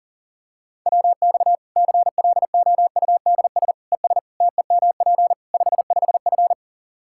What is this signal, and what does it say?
Morse code, heard as a keyed tone, about 40 wpm, 700 Hz. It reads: WX CLOUDS ES TEMP 55F
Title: Practice QSO - Exchange 2 - Info - 40wpm